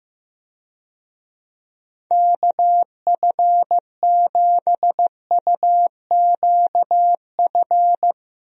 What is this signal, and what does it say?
Morse code, heard as a keyed tone, about 15 wpm, 700 Hz. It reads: KF7UQF
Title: USA Call Signs - 6 Characters - Repeated 2x - 15wpm